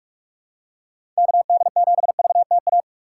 Morse code, heard as a keyed tone, about 46 wpm, 700 Hz. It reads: KD7VTA
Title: USA Call Signs - 6 Characters - 4x Character Spacing - 45wpm